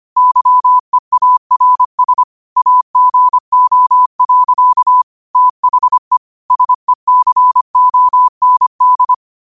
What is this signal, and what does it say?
Morse code, heard as a keyed tone, about 25 wpm, 1000 Hz. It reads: YEARS AGO. THE SECOND
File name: qotdfiles / 25wpm